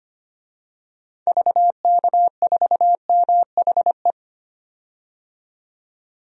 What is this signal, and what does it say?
Morse code, heard as a keyed tone, about 25 wpm, 700 Hz. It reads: VK4MHE
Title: DXCC-100 Call Signs - 6 Characters - 25wpm